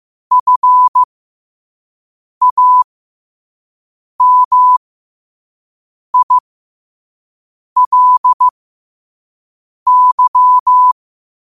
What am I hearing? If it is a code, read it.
Morse code, heard as a keyed tone, about 15 wpm, spaced as for 5 wpm, 1000 Hz. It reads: FAMILY
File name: qotdfiles / 5wpm